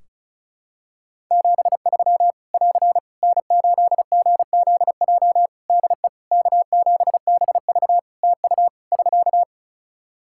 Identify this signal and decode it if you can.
Morse code, heard as a keyed tone, about 34 wpm, 700 Hz. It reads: 73 <AR> N8GZJ DE K7BV TU <SK>
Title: Practice QSO - Exchange 4 - Ending - 35wpm